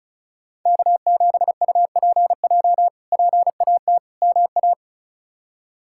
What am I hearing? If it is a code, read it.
Morse code, heard as a keyed tone, about 36 wpm, 700 Hz. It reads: K7UPJ PAT MA
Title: K1USN SST QSO Practice - 35wpm